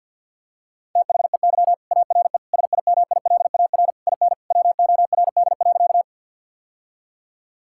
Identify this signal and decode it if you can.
Morse code, heard as a keyed tone, about 50 wpm, 700 Hz. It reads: THEY ARE SIMILAR IN WORD.
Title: Sentences from Top 1000 Words - Review Part 5 - 50wpm